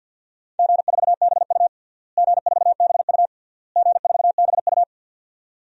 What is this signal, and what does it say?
Morse code, heard as a keyed tone, about 52 wpm, 700 Hz. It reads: G4BU G4BU G4BU
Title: DXCC-100 Call Signs - 4 Characters - Repeated 3x - 50wpm